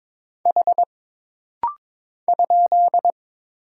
Morse code, heard as a keyed tone, about 22 wpm, 700 Hz. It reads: H ?
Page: This rapid-fire timing is geared to push your ICR (Instant Character Recognition) skills.